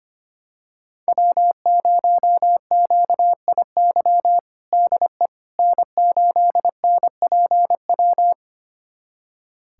Morse code, heard as a keyed tone, about 25 wpm, 700 Hz. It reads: W0QIY DE N8NPW
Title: USA Callsign DE Callsign - 5 Characters - 25wpm